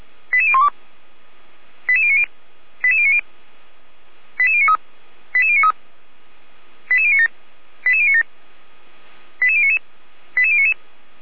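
Five-tone selective call (ZVEI), heard as quick runs of five tones, tones of about 80 ms. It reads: 88012 88090 88090 88093 88093 88097 88097 88099 88099
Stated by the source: ZVEI five-tone received by SDR, fire brigade Feuerwehr Orlamünde.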